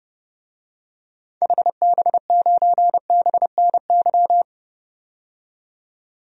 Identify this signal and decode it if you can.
Morse code, heard as a keyed tone, about 30 wpm, 700 Hz. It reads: HB9BNY
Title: DXCC-100 Call Signs - 6 Characters - 30wpm